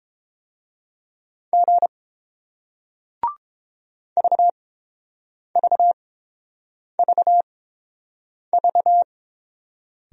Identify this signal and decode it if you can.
Morse code, heard as a keyed tone, about 32 wpm, spaced as for 7 wpm, 700 Hz. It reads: G VVVV